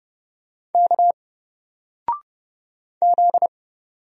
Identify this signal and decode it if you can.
Morse code, heard as a keyed tone, about 31 wpm, 700 Hz. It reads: K Z
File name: Single-Letter-Number-30wpm